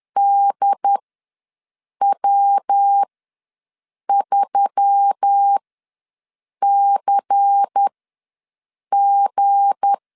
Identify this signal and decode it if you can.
Morse code, heard as a keyed tone, about 10 wpm, spaced as for 6 wpm, 800 Hz. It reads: DW3CG
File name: LEZIONI / 7 GF9Z / 7 G ws 10wpm 3843